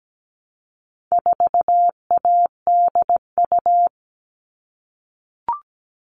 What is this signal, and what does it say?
Morse code, heard as a keyed tone, about 17 wpm, 700 Hz. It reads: 4ADU
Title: All Cut Numbers - 4-Digit 17wpm